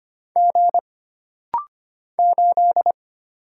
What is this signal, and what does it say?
Morse code, heard as a keyed tone, about 25 wpm, 700 Hz. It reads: G 8
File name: Single Character (Letter:Number:Punctuation:Prosign) - Rapid-Fire - 25wpm